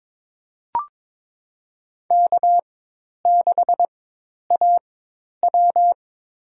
Morse code, heard as a keyed tone, about 22 wpm, spaced as for 10 wpm, 700 Hz. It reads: K6AW
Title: USA Call Signs - 4x Character Spacing - 22wpm